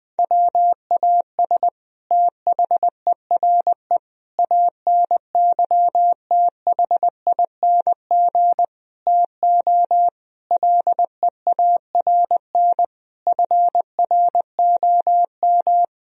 Morse code, heard as a keyed tone, about 21 wpm, 700 Hz. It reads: WAS THERE ANYTHING TO LEARN FROM
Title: Sentences from Top 1000 Words - Review Part 1 - 20wpm